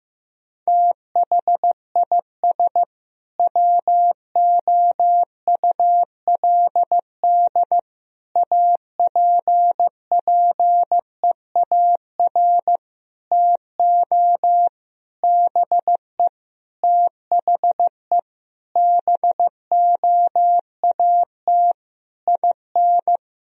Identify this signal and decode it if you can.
Morse code, heard as a keyed tone, about 15 wpm, 700 Hz. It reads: THIS WOULD APPEAR TO BE THE BOAT IN
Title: Sentences from Top 1000 Words - Review Part 3 - 15wpm